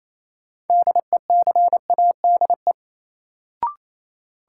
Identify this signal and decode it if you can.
Morse code, heard as a keyed tone, about 28 wpm, 700 Hz. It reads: DECADE
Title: Sets of 5 Words - New Word Drill - 28wpm